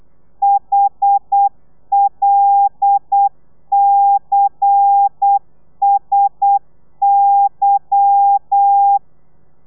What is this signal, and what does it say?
Morse code, heard as a keyed tone, about 8 wpm, 800 Hz. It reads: HLCSY
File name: telegrafia / i2tfj / Lez10 35